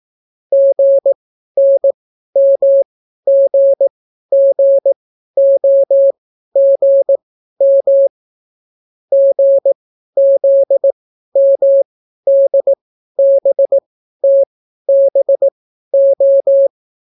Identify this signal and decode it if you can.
Morse code, heard as a keyed tone, about 18 wpm, spaced as for 12 wpm, 550 Hz. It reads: GNMGGOGM GZMDBTBO